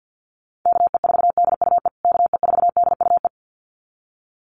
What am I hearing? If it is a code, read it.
Morse code, heard as a keyed tone, about 50 wpm, 700 Hz. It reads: KE4DUE KE4DUE
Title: USA Call Signs Repeated 2x 50wpm